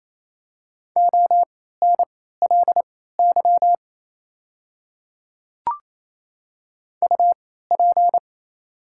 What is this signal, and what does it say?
Morse code, heard as a keyed tone, about 28 wpm, spaced as for 16 wpm, 700 Hz. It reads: ONLY UP
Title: 3x Character Spacing - Top 300 Words - 28wpm